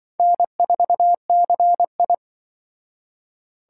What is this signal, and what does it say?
Morse code, heard as a keyed tone, about 24 wpm, 700 Hz. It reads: N4CI
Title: Speed Racing - USA 1x2 Call Signs 20wpm